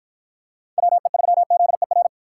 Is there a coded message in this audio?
Yes: Morse code from 0.8 to 2.1 s, heard as a keyed tone, about 53 wpm, 700 Hz. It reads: WE3ZER